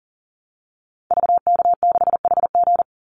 Morse code, heard as a keyed tone, about 40 wpm, 700 Hz. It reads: VK6HG
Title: Call Signs Repeated 3x - 40wpm